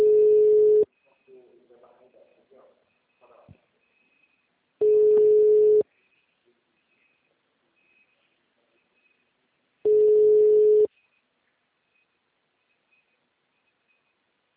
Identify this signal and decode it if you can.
Ringback tone (Europe, 425 Hz), with speech.